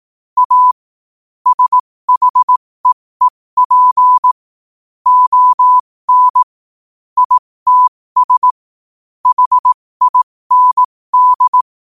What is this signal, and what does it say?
Morse code, heard as a keyed tone, about 18 wpm, spaced as for 15 wpm, 1000 Hz. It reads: A SHEEP ON ITS HIND
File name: qotdfiles / 15wpm